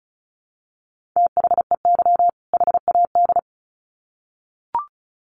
Morse code, heard as a keyed tone, about 34 wpm, 700 Hz. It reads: THEY HAD
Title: Sets of 2 Words 35wpm